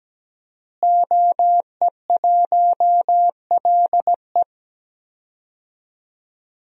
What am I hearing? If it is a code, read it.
Morse code, heard as a keyed tone, about 17 wpm, 700 Hz. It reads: OE1LE
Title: DXCC-100 Call Signs - 5 Characters - Repeated 3x - 17wpm